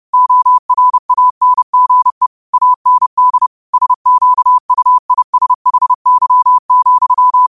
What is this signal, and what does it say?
Morse code, heard as a keyed tone, about 30 wpm, 1000 Hz. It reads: ORANGE AND SQUISHY,
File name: qotdfiles / 30wpm